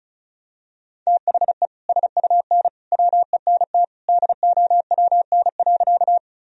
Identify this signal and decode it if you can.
Morse code, heard as a keyed tone, about 36 wpm, 700 Hz. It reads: THE SUN WENT DOWN.
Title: Sentences from Top 1000 Words - Review Part 1 - 35wpm